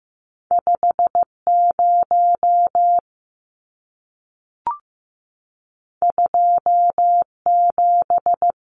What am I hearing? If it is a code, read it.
Morse code, heard as a keyed tone, about 15 wpm, 700 Hz. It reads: 5 0 2 7
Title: Numbers - 2-Digit 15wpm